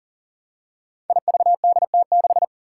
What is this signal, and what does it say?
Morse code, heard as a keyed tone, about 40 wpm, 700 Hz. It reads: IVDT6
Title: Five Challenging Letter-Number Combinations 40wpm